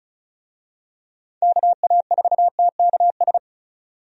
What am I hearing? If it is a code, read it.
Morse code, heard as a keyed tone, about 36 wpm, 700 Hz. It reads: KA4TKS
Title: USA Call Signs - 6 Characters - 35wpm